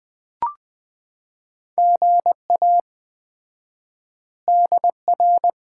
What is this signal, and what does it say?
Morse code, heard as a keyed tone, about 20 wpm, 700 Hz. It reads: GA DR
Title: Common QSO Phrases - 4x Word-Spacing - 20wpm